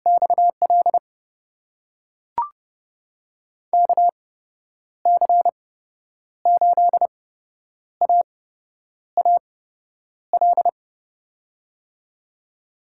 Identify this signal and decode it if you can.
Morse code, heard as a keyed tone, about 30 wpm, spaced as for 9 wpm, 700 Hz. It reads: XL KC8AAL
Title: USA Call Signs - 6 Characters - 8x Character Spacing - 30wpm